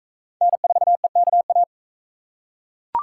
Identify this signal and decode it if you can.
Morse code, heard as a keyed tone, about 39 wpm, 700 Hz. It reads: N4EKA